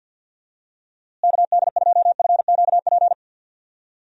Morse code, heard as a keyed tone, about 52 wpm, 700 Hz. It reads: KD2FQP
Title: USA Call Signs - 6 Characters - Repeated 3x - 50wpm